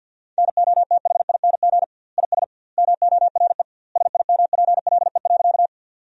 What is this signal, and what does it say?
Morse code, heard as a keyed tone, about 50 wpm, 700 Hz. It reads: NOTHING IS MORE SIMPLE.
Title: 2x Word-Spacing - Sentences from Top 500 Words - 50wpm